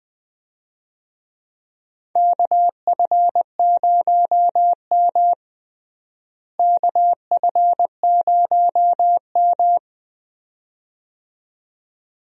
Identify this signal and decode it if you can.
Morse code, heard as a keyed tone, about 21 wpm, 700 Hz. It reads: KF0M KF0M
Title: USA Call Signs - 4 Characters - Repeated 2x - 20wpm